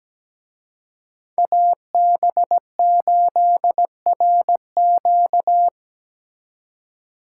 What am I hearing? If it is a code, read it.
Morse code, heard as a keyed tone, about 18 wpm, 700 Hz. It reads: AB8RQ
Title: Speed Racing - USA Call Signs 17wpm